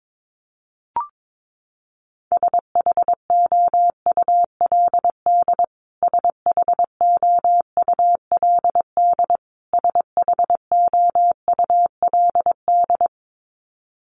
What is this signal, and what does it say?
Morse code, heard as a keyed tone, about 22 wpm, 700 Hz. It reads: SHOULD SHOULD SHOULD
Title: Top 500 Words Repeated 22wpm